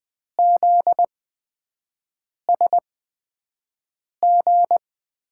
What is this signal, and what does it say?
Morse code, heard as a keyed tone, about 20 wpm, spaced as for 5 wpm, 700 Hz. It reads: ZSG
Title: USA Call Signs - 6 Characters - 8x Character Spacing - 20wpm